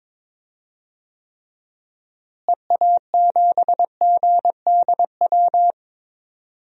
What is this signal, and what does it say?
Morse code, heard as a keyed tone, about 22 wpm, 700 Hz. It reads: EA7GDW
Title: DXCC-100 Call Signs - 6 Characters - Repeated 2x - 22wpm